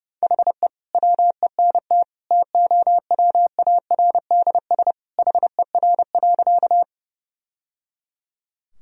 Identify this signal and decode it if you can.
Morse code, heard as a keyed tone, about 32 wpm, 700 Hz. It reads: HE WENT TOWARDS HER.
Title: Sentences from Top 1000 Words - Review Part 2 - 30wpm